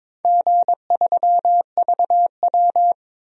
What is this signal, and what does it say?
Morse code, heard as a keyed tone, about 22 wpm, 700 Hz. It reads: G3VW
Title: DXCC-100 Call Signs - 4 Characters - Repeated 3x - 22wpm